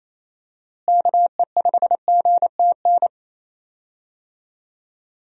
Morse code, heard as a keyed tone, about 28 wpm, 700 Hz. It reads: KE5GTN
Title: USA Call Signs - 6 Characters - Repeated 2x - 28wpm